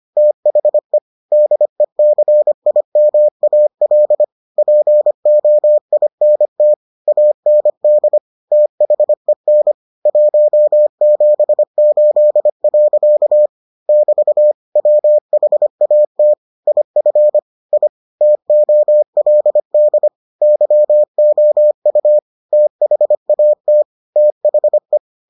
Morse code, heard as a keyed tone, about 25 wpm, 600 Hz. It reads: THE DECIMAL POINT AND THEN 178. <BT> WHAT IF I TOLD YOU THAT THE